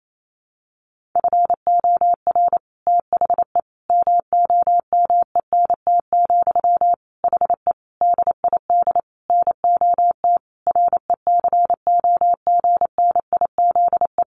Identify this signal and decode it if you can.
Morse code, heard as a keyed tone, about 28 wpm, 700 Hz. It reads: FOR THE MOMENT, HE DID NOT RECOGNIZE
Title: Sentences from Top 1400 Words 28wpm